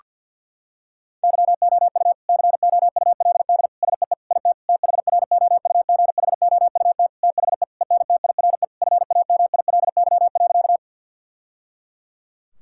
Morse code, heard as a keyed tone, about 50 wpm, 700 Hz. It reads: YOU COULD SEE IT THROUGHOUT THE ENTIRE FAMILY.
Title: Sentences from Top 1000 Words - Review Part 5 - 50wpm